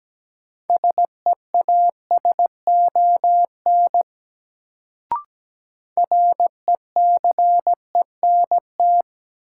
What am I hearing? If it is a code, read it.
Morse code, heard as a keyed tone, about 17 wpm, 700 Hz. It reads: SEASON RECENT
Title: Sentences from Top 1400 Words - New Word Drill Alt - 17wpm